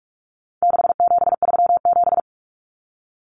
Morse code, heard as a keyed tone, about 44 wpm, 700 Hz. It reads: 6737
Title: Common Cut Numbers - 4-Digit 45wpm